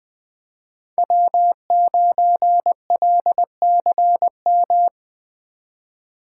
Morse code, heard as a keyed tone, about 20 wpm, 700 Hz. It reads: W9LCM
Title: USA Call Signs - 5 Characters - 4x Character Spacing - 20wpm